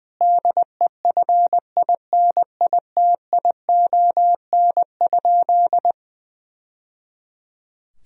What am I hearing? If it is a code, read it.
Morse code, heard as a keyed tone, about 19 wpm, 700 Hz. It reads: DEFINITION?
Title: Sentences from Top 1000 Words - Review Part 4 - 20wpm